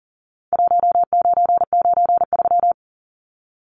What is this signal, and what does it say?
Morse code, heard as a keyed tone, about 40 wpm, 700 Hz. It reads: 1993
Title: Numbers - Contest Style - 4-Digit 40wpm